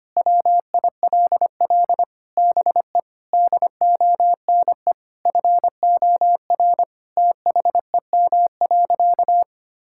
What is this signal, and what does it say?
Morse code, heard as a keyed tone, about 25 wpm, 700 Hz. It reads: WILL BE DONE FOR THEM.
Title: Sentences from Top 1600 Words 25wpm